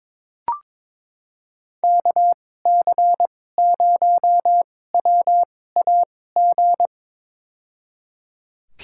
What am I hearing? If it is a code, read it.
Morse code, heard as a keyed tone, about 22 wpm, spaced as for 16 wpm, 700 Hz. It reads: KC0WAG
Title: USA Call Signs - 6 Characters - 2x Character Spacing - 22wpm